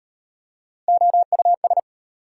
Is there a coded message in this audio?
Yes: Morse code from 0.9 to 1.8 s, heard as a keyed tone, about 37 wpm, 690 Hz. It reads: OUS